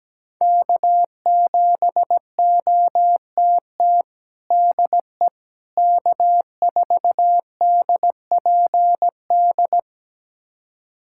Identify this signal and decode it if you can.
Morse code, heard as a keyed tone, about 17 wpm, 700 Hz. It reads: K7OTT DE K4DPD
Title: USA Callsign DE Callsign - 5 Characters - 17wpm